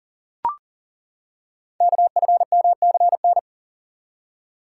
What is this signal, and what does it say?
Morse code, heard as a keyed tone, about 39 wpm, 700 Hz. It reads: KFMCN
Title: Code Groups - 5 Letters - 40wpm